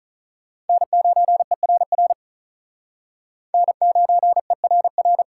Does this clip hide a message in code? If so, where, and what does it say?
Yes: Morse code from 0.7 to 5.2 s, heard as a keyed tone, about 45 wpm, 700 Hz. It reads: N9ERR N9ERR